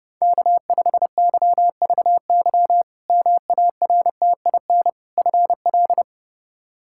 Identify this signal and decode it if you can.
Morse code, heard as a keyed tone, about 30 wpm, 700 Hz. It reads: K5YVY MARTIN FL
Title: K1USN SST QSO Practice - 30wpm